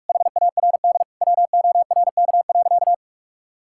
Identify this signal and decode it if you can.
Morse code, heard as a keyed tone, about 46 wpm, 700 Hz. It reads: HARD WORK.
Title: Sentences from Top 1000 Words - Review Part 5 - 45wpm